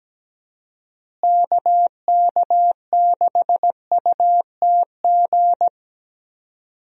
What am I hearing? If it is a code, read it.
Morse code, heard as a keyed tone, about 17 wpm, 700 Hz. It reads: KK6UTG
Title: USA Call Signs - 6 Characters - Repeated 3x - 17wpm